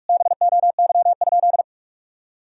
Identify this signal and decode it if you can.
Morse code, heard as a keyed tone, about 46 wpm, 700 Hz. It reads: BOY?
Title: Sentences from Top 1000 Words - Review Part 1 - 45wpm